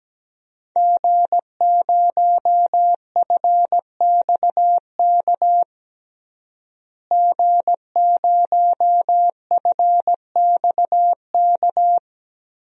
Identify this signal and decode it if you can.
Morse code, heard as a keyed tone, about 17 wpm, 700 Hz. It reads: G0FXK G0FXK
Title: DXCC-100 Call Signs - 5 Characters - Repeated 2x - 17wpm